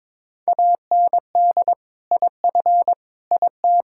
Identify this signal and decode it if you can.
Morse code, heard as a keyed tone, about 21 wpm, 700 Hz. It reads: AND IF IT
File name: practice / 22wpm / Sentences with 2-Word Phrases - 22wpm